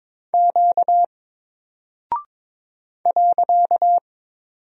Morse code, heard as a keyed tone, about 22 wpm, 700 Hz. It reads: Q .
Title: Single Character (Letter:Number:Punctuation:Prosign) 22wpm